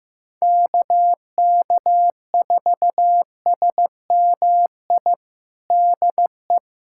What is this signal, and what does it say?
Morse code, heard as a keyed tone, about 15 wpm, 700 Hz. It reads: KK4SMI DE
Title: USA Callsign DE Callsign - 6 Characters - 15wpm